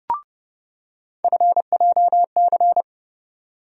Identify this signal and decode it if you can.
Morse code, heard as a keyed tone, about 30 wpm, 700 Hz. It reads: FJC
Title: USA Call Sign Postfixes - 3 Letters 30wpm